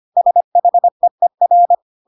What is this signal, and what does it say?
Morse code, heard as a keyed tone, about 25 wpm, 700 Hz. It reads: SHEER
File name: Pinocchio / Pinocchio ch3 25 25 ik1hge